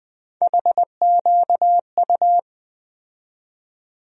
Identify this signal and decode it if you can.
Morse code, heard as a keyed tone, about 21 wpm, 700 Hz. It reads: HQU
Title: USA Call Sign Postfixes - 3 Letters 20wpm